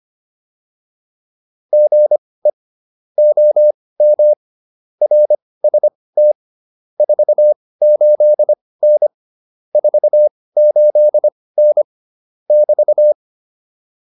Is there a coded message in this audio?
Yes: Morse code from 1.7 to 13.1 s, heard as a keyed tone, about 25 wpm, spaced as for 18 wpm, 600 Hz. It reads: GE OM RST 48N 48N <BT>